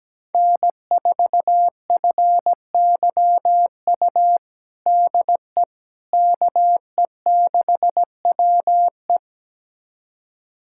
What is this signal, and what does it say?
Morse code, heard as a keyed tone, about 17 wpm, 700 Hz. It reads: N4FYU DE KE6WE
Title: USA Callsign DE Callsign - 5 Characters - 17wpm